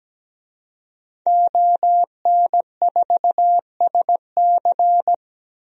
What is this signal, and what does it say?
Morse code, heard as a keyed tone, about 17 wpm, 700 Hz. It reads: ON4SC
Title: DXCC-100 Call Signs - 5 Characters - 17wpm